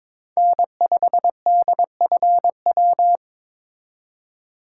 Morse code, heard as a keyed tone, about 23 wpm, 700 Hz. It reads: N5DFW
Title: USA Call Signs - 5 Characters - 2x Character Spacing - 22wpm